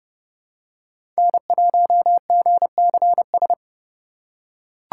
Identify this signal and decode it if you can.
Morse code, heard as a keyed tone, about 30 wpm, 700 Hz. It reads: N1GCS
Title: USA Call Signs - 6x Character Spacing - 30wpm